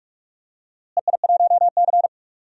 Morse code, heard as a keyed tone, about 46 wpm, 700 Hz. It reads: EI1C
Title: DXCC-100 Call Signs - 4 Characters - Repeated 2x - 45wpm